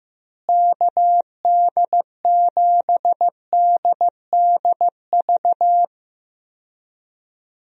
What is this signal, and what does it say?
Morse code, heard as a keyed tone, about 15 wpm, 700 Hz. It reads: KD7DDV
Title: USA Call Signs - 6 Characters - Repeated 2x - 15wpm